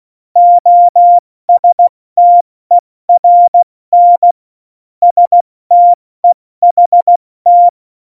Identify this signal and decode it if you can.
Morse code, heard as a keyed tone, about 16 wpm, 700 Hz. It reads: OSTERN STEHT
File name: Morsekurs / Witze / Witz 12 Mai, 2020 70 F80